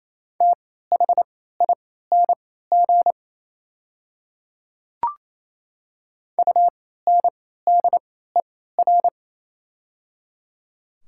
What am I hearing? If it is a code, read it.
Morse code, heard as a keyed tone, about 27 wpm, spaced as for 16 wpm, 700 Hz. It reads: THING UNDER
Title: 3x Character Spacing - Top 300 Words - 28wpm